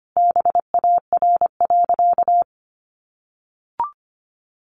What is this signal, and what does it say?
Morse code, heard as a keyed tone, about 25 wpm, 700 Hz. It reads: BAR.
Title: Easy Phrases 25wpm